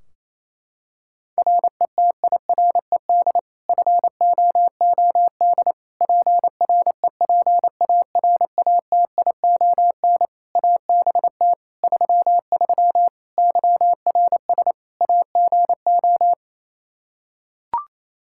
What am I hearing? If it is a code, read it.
Morse code, heard as a keyed tone, about 27 wpm, 700 Hz. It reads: RETIRED FOOD PREPARATION ABT 33 YRS AGO
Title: K1USN SST QSO Practice 28wpm